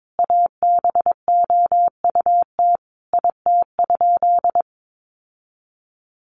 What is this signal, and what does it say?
Morse code, heard as a keyed tone, about 22 wpm, 700 Hz. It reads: ABOUT IT?
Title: Sentences from Top 1500 Words 22wpm